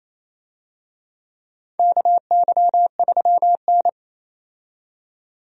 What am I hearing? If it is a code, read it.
Morse code, heard as a keyed tone, about 28 wpm, 700 Hz. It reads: KY3N
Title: USA Call Signs - 4 Characters - 28wpm